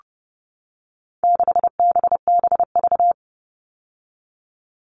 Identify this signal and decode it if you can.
Morse code, heard as a keyed tone, about 30 wpm, 700 Hz. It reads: 6BBV
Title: All Cut Numbers - 4-Digit 30wpm